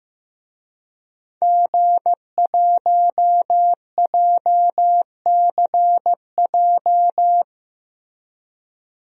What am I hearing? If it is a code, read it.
Morse code, heard as a keyed tone, about 15 wpm, 700 Hz. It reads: G1JCJ
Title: DXCC-100 Call Signs - 5 Characters - Repeated 3x - 15wpm